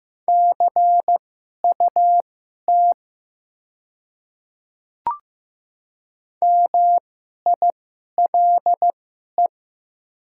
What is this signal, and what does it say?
Morse code, heard as a keyed tone, about 15 wpm, spaced as for 11 wpm, 690 Hz. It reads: CUT MILE
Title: 2x Character Spacing - Top 300 Words - 15wpm